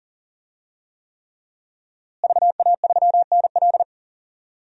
Morse code, heard as a keyed tone, about 37 wpm, 700 Hz. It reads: VA3NL
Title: DXCC-100 Call Signs - 5 Characters - Repeated 3x - 40wpm